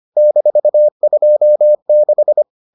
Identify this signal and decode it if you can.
Morse code, heard as a keyed tone, about 25 wpm, 600 Hz. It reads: -26